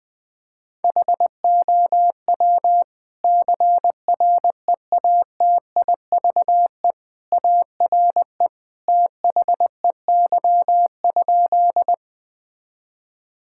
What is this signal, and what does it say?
Morse code, heard as a keyed tone, about 20 wpm, 700 Hz. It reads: HOW CREATIVE ARE THEY?
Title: Sentences from Top 1000 Words - Review Part 5 - 20wpm